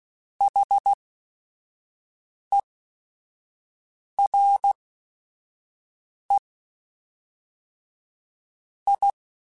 Morse code, heard as a keyed tone, about 16 wpm, spaced as for 5 wpm, 800 Hz. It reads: HERE I